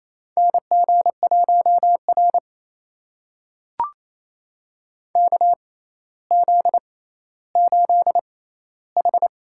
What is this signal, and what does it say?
Morse code, heard as a keyed tone, about 28 wpm, spaced as for 14 wpm, 700 Hz. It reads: NG1R KZ8H
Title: USA Call Signs - 4 Characters - 6x Character Spacing - 28wpm